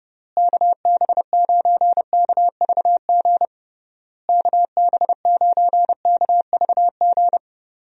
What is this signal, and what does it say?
Morse code, heard as a keyed tone, about 30 wpm, 700 Hz. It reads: KB9KVG KB9KVG
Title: USA Call Signs - 6 Characters - Repeated 3x - 30wpm